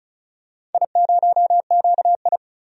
Morse code, heard as a keyed tone, about 36 wpm, 690 Hz. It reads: I0QI
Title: DXCC-100 Call Signs - 4 Characters - 35wpm